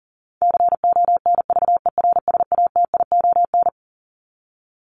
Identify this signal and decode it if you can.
Morse code, heard as a keyed tone, about 40 wpm, 700 Hz. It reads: CONVERSATION
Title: Top 1751-2000 Words 40wpm